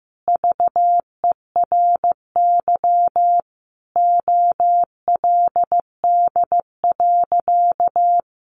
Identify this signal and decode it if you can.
Morse code, heard as a keyed tone, about 15 wpm, 700 Hz. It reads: VERY OLD.
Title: Sentences from Top 1100 Words 15wpm